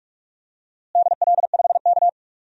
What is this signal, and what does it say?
Morse code, heard as a keyed tone, about 41 wpm, 700 Hz. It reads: DL5K
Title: DXCC-100 Call Signs - 4 Characters - Repeated 2x - 45wpm